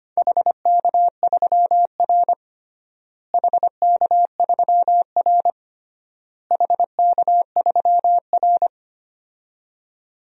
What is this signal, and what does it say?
Morse code, heard as a keyed tone, about 25 wpm, 700 Hz. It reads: HK3R HK3R HK3R
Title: DXCC-100 Call Signs - 4 Characters - Repeated 3x - 25wpm